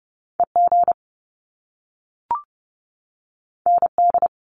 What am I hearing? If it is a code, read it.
Morse code, heard as a keyed tone, about 30 wpm, 700 Hz. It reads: EG ND
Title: Top Two Letter Combinations in Words 30wpm